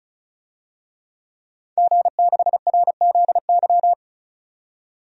Morse code, heard as a keyed tone, about 36 wpm, 700 Hz. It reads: G6RZY
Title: DXCC-100 Call Signs - 5 Characters - 35wpm